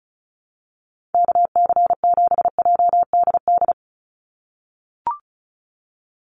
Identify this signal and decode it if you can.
Morse code, heard as a keyed tone, about 36 wpm, 700 Hz. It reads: KC7JDD
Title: USA Call Signs 35wpm